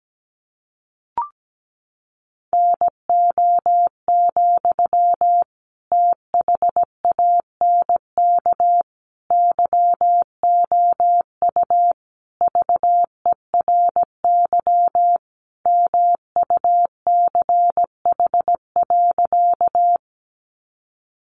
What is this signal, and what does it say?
Morse code, heard as a keyed tone, about 17 wpm, 700 Hz. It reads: NO, THANK YOU VERY MUCH.
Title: Sentences from Top 600 Words 17wpm